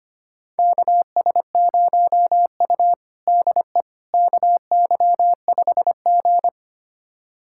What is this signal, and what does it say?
Morse code, heard as a keyed tone, about 25 wpm, 700 Hz. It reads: KS0U DE KY5G
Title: USA Callsign DE Callsign - 4 Characters - 25wpm